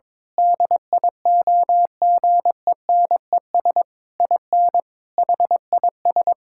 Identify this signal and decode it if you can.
Morse code, heard as a keyed tone, about 22 wpm, 700 Hz. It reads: DIOGENES IN HIS
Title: Pangram Challenge v2 - 22wpm